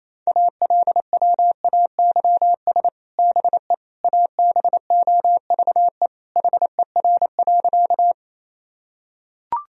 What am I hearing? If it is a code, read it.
Morse code, heard as a keyed tone, about 28 wpm, 700 Hz. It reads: ALWAYS BE ABOVE HER.
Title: Sentences from Top 1000 Words - Review Part 2 - 28wpm